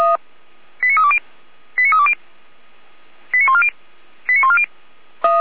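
Five-tone selective call (ZVEI), heard as quick runs of five tones, tones of about 70 ms. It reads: 89320 89320 89140 89140
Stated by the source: Five-tone ZVEI selective call received on an SDR.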